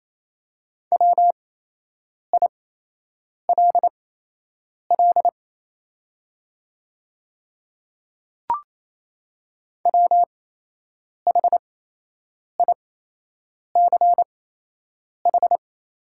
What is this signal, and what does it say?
Morse code, heard as a keyed tone, about 28 wpm, spaced as for 8 wpm, 700 Hz. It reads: WILL WHICH
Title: 8x Character Spacing - Top 500 Words - 28wpm